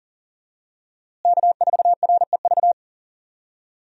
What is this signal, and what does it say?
Morse code, heard as a keyed tone, about 40 wpm, 700 Hz. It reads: K4REV
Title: USA Call Signs - 5 Characters - Repeated 2x - 40wpm